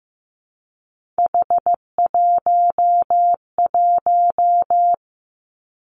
Morse code, heard as a keyed tone, about 15 wpm, 700 Hz. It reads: H11
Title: Three Challenging Letter-Number Combinations 15wpm